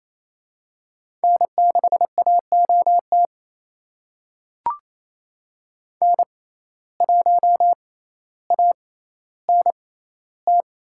Morse code, heard as a keyed tone, about 28 wpm, spaced as for 14 wpm, 700 Hz. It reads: N6AOT N1ANT
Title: USA Call Signs - 5 Characters - 6x Character Spacing - 28wpm